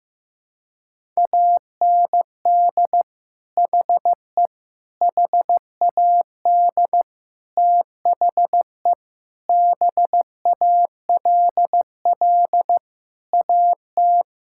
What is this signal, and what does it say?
Morse code, heard as a keyed tone, about 15 wpm, 700 Hz. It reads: AND HE HAD THE BALL AT
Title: Sentences from Top 1000 Words - Review Part 6 - 15wpm